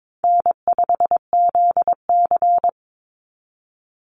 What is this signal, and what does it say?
Morse code, heard as a keyed tone, about 22 wpm, 700 Hz. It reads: N5ZC
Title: NA SOTA Chaser Callsigns 22wpm